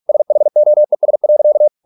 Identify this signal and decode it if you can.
Morse code, heard as a keyed tone, about 46 wpm, 600 Hz. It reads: SHOES.